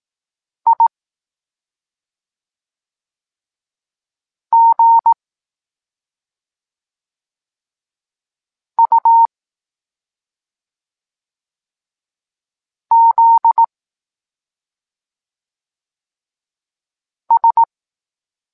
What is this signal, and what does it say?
Morse code, heard as a keyed tone, about 18 wpm, 920 Hz. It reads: I G U Z S
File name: Tekst-26 1